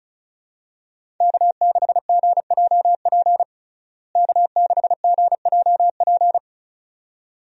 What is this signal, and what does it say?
Morse code, heard as a keyed tone, about 34 wpm, 700 Hz. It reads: K6GJP K6GJP
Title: USA Call Signs - 5 Characters - Repeated 2x - 35wpm